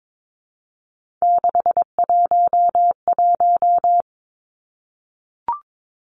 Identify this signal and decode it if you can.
Morse code, heard as a keyed tone, about 22 wpm, 700 Hz. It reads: 611
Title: Three Challenging Letter-Number Combinations 22wpm